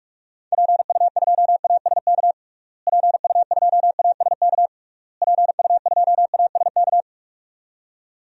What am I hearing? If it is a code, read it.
Morse code, heard as a keyed tone, about 41 wpm, 700 Hz. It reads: PU2ASK PU2ASK PU2ASK
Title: DXCC-100 Call Signs - 6 Characters - Repeated 3x - 45wpm